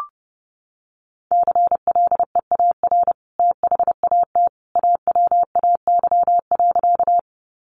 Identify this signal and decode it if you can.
Morse code, heard as a keyed tone, about 30 wpm, 700 Hz. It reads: CLEAR THAT AWAY.
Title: Sentences from Top 600 Words 30wpm